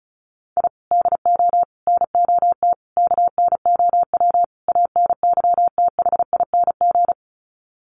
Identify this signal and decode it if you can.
Morse code, heard as a keyed tone, about 34 wpm, 700 Hz. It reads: I DO NOT KNOW ANYTHING
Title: Sets of 5 Words Encore 35wpm